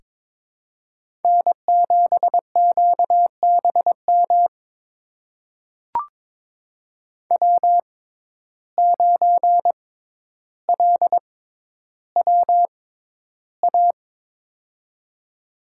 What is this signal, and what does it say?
Morse code, heard as a keyed tone, about 22 wpm, spaced as for 11 wpm, 700 Hz. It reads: N7QBM W9LWA
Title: USA Call Signs - 5 Characters - 6x Character Spacing - 22wpm